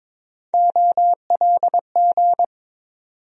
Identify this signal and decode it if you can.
Morse code, heard as a keyed tone, about 22 wpm, 700 Hz. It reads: OLG
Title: USA Call Sign Postfixes - 3 Letters 22wpm